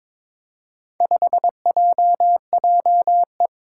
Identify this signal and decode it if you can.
Morse code, heard as a keyed tone, about 21 wpm, 700 Hz. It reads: 5JJE
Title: Four Challenging Letter-Number Combinations 22wpm